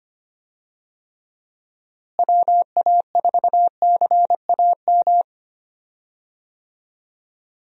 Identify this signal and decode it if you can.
Morse code, heard as a keyed tone, about 25 wpm, 700 Hz. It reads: WA4CAM
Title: USA Call Signs - 6 Characters - Repeated 2x - 25wpm